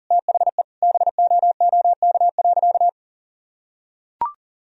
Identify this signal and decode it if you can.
Morse code, heard as a keyed tone, about 37 wpm, 700 Hz. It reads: THE BOOK.
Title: Sentences from Top 1000 Words - Review Part 1 - 40wpm